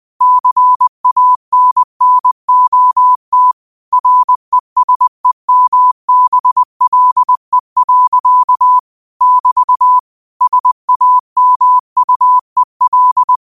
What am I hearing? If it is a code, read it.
Morse code, heard as a keyed tone, about 20 wpm, 1000 Hz. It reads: CANNOT RESEMBLE. <BT> SAMUEL
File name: qotdfiles / 20wpm